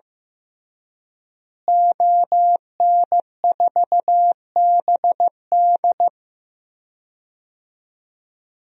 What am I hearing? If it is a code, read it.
Morse code, heard as a keyed tone, about 15 wpm, 700 Hz. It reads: ON4BD
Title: DXCC-100 Call Signs - 5 Characters - Repeated 2x - 15wpm